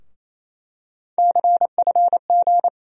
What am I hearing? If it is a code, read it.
Morse code, heard as a keyed tone, about 28 wpm, 700 Hz. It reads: CFG
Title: USA Call Sign Postfixes - 3 Letters 28wpm